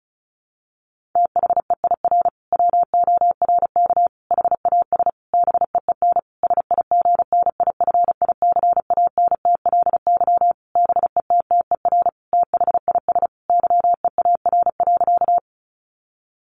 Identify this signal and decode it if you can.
Morse code, heard as a keyed tone, about 34 wpm, 700 Hz. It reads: THEIR WORK HAS BEEN SIGNIFICANTLY BETTER THIS YEAR.
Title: Sentences from Top 1500 Words 35wpm